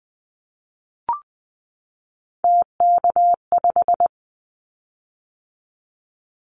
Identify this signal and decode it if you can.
Morse code, heard as a keyed tone, about 20 wpm, 700 Hz. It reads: TK5
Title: Callsign Prefixes for Top 100 DXCC - 20wpm